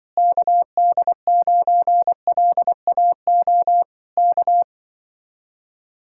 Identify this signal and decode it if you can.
Morse code, heard as a keyed tone, about 24 wpm, 700 Hz. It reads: KD9LAO K
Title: Speed-Racing Common QSO Phrases - 20wpm